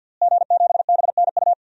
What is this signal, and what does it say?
Morse code, heard as a keyed tone, about 52 wpm, 700 Hz. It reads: G7BNU
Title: DXCC-100 Call Signs - 5 Characters - Repeated 3x - 50wpm